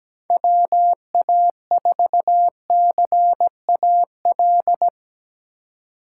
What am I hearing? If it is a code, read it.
Morse code, heard as a keyed tone, about 19 wpm, 700 Hz. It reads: WA4CAL
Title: Speed Racing - USA 2x3 Call Signs 17wpm